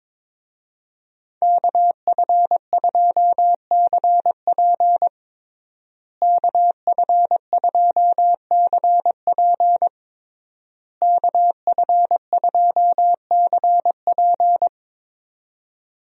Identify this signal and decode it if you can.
Morse code, heard as a keyed tone, about 22 wpm, 700 Hz. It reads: KF2CP KF2CP KF2CP
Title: USA Call Signs - 5 Characters - Repeated 3x - 22wpm